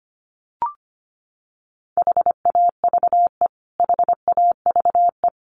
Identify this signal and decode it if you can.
Morse code, heard as a keyed tone, about 25 wpm, 700 Hz. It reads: HAVE HAVE
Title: Top 300 Words Repeated 25wpm